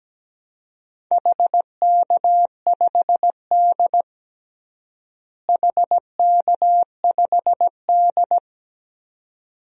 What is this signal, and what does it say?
Morse code, heard as a keyed tone, about 17 wpm, 700 Hz. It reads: HK5D HK5D
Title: DXCC-100 Call Signs - 4 Characters - Repeated 2x - 17wpm